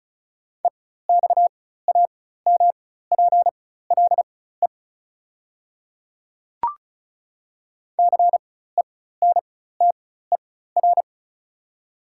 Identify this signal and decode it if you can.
Morse code, heard as a keyed tone, about 36 wpm, spaced as for 16 wpm, 700 Hz. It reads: EXAMPLE CENTER
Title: 4x Character Spacing - Top 500 Words - 35wpm